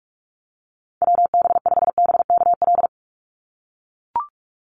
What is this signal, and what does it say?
Morse code, heard as a keyed tone, about 44 wpm, 700 Hz. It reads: WB5BKL
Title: Call Signs - 45wpm